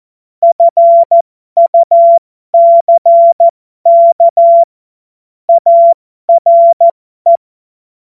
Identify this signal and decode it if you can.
Morse code, heard as a keyed tone, about 14 wpm, spaced as for 12 wpm, 680 Hz. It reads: FUCK ARE